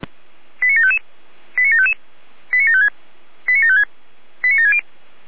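Five-tone selective call (ZVEI), heard as quick runs of five tones, tones of about 70 ms. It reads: 89755 89755 89756 89756 89760
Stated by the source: Five-tone ZVEI selective call received by SDR, address 89720.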